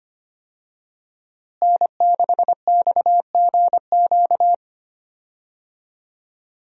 Morse code, heard as a keyed tone, about 25 wpm, 700 Hz. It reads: N6XGQ
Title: USA Call Signs - 5 Characters - Repeated 3x - 25wpm